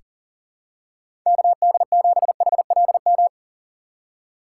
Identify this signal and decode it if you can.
Morse code, heard as a keyed tone, about 40 wpm, 700 Hz. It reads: KD7HLM
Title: USA Call Signs - 6 Characters - 6x Character Spacing - 40wpm